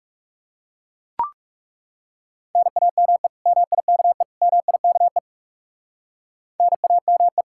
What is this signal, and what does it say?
Morse code, heard as a keyed tone, about 37 wpm, 700 Hz. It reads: NAME MIKE MIKE NAME
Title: Speed-Racing Common QSO Phrases - 30wpm